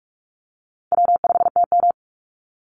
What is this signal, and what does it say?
Morse code, heard as a keyed tone, about 44 wpm, 700 Hz. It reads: W5TM
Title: Easy Call Signs Repeated 2x - 45wpm